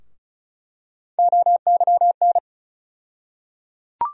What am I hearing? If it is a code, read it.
Morse code, heard as a keyed tone, about 34 wpm, 700 Hz. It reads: OYN
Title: USA Call Sign Postfixes - 3 Letters 35wpm